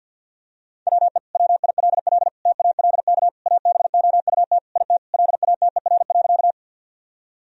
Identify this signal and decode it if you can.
Morse code, heard as a keyed tone, about 50 wpm, 700 Hz. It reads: WE WILL TALK ABOUT IT LATER.
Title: Sentences from Top 1000 Words - Review Part 4 - 50wpm